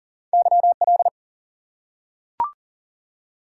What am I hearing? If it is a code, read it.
Morse code, heard as a keyed tone, about 40 wpm, 700 Hz. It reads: YL
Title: Top 180 QSO Elements 40wpm